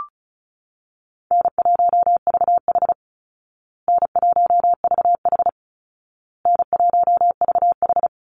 Morse code, heard as a keyed tone, about 34 wpm, 700 Hz. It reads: N1VH N1VH N1VH
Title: Easy Call Signs Repeated 3x - 35wpm